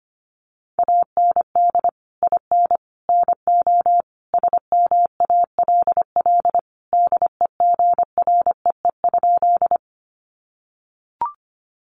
Morse code, heard as a keyed tone, about 25 wpm, 700 Hz. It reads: AND IN NO SMALL DEGREE?
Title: Sentences from Top 1500 Words 25wpm